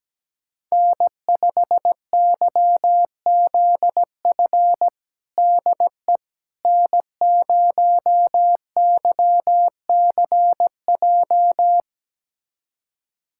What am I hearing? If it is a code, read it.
Morse code, heard as a keyed tone, about 17 wpm, 700 Hz. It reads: N5YZF DE N0YCJ
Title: USA Callsign DE Callsign - 5 Characters - 17wpm